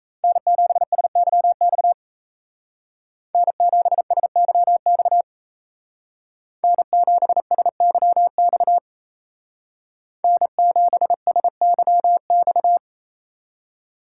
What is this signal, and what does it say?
Morse code, heard as a keyed tone, about 35 wpm, 700 Hz. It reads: N7SYX N7SYX N7SYX N7SYX